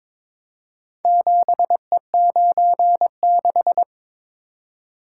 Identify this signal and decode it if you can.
Morse code, heard as a keyed tone, about 23 wpm, 700 Hz. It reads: 7E96
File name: practice / 22wpm / Four Letter-Number Combinations - 22wpm